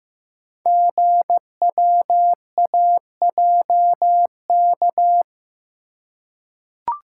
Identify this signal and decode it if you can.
Morse code, heard as a keyed tone, about 15 wpm, 700 Hz. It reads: GWAJK
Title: Code Groups - 5 Letter:Numbers - 15wpm